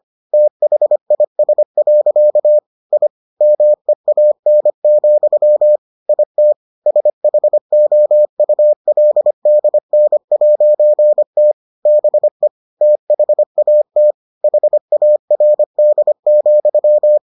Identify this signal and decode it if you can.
Morse code, heard as a keyed tone, about 25 wpm, 600 Hz. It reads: THIS. I MEAN, IT SHOULDN'T BE THAT HARD,